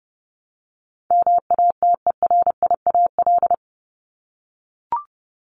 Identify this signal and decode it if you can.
Morse code, heard as a keyed tone, about 30 wpm, 700 Hz. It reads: MATERIAL
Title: ES:S Suffix - Top 100 Words with Base Words 30wpm